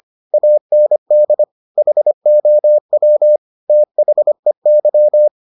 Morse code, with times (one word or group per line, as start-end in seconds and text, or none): 0.3-1.4 AND
1.8-3.4 HOW
3.7-5.3 THEY